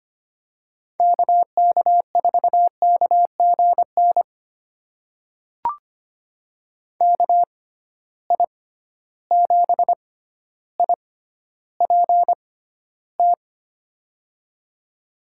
Morse code, heard as a keyed tone, about 25 wpm, spaced as for 13 wpm, 700 Hz. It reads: KK4KGN KI7IPT
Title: USA Call Signs - 6 Characters - 6x Character Spacing - 25wpm